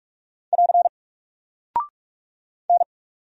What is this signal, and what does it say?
Morse code, heard as a keyed tone, about 44 wpm, 700 Hz. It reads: <AR> N
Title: Single Character (Letter:Number:Punctuation:Prosign) Alt 45wpm